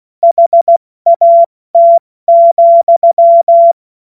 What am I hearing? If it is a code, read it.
Morse code, heard as a keyed tone, about 16 wpm, spaced as for 14 wpm, 680 Hz. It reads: HAT,
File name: Morsekurs / Witze / Witz 23. Dezember 2025 70 F80